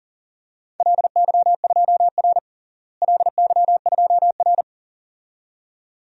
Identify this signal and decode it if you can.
Morse code, heard as a keyed tone, about 37 wpm, 700 Hz. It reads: LY2R LY2R
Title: DXCC-100 Call Signs - 4 Characters - Repeated 2x - 40wpm